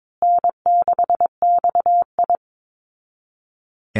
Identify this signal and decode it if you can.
Morse code, heard as a keyed tone, about 22 wpm, 700 Hz. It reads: N6XI
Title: Easy Call Signs Repeated 3x - 22wpm